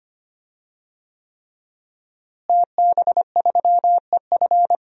Morse code, heard as a keyed tone, about 25 wpm, 700 Hz. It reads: TB3EF
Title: DXCC-100 Call Signs - 5 Characters - Repeated 3x - 25wpm